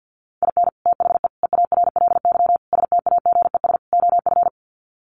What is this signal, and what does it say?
Morse code, heard as a keyed tone, about 50 wpm, 700 Hz. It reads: IN THE EARLY STAGES OF
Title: Sets of 5 Words Encore 50wpm